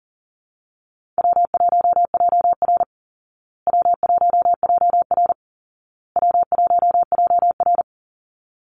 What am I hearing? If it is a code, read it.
Morse code, heard as a keyed tone, about 40 wpm, 700 Hz. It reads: W1JR W1JR W1JR
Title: Call Signs Repeated 2x - 40wpm